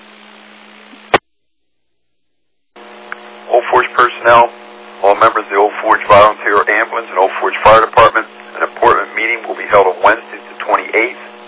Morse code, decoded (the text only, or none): none